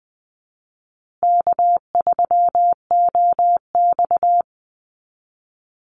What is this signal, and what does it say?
Morse code, heard as a keyed tone, about 19 wpm, 700 Hz. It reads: K3OX
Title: Easy Call Signs Repeated 2x - 20wpm